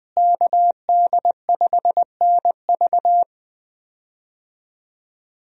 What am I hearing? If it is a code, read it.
Morse code, heard as a keyed tone, about 21 wpm, 700 Hz. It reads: KD5NV
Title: USA Call Signs - 5 Characters - Repeated 3x - 20wpm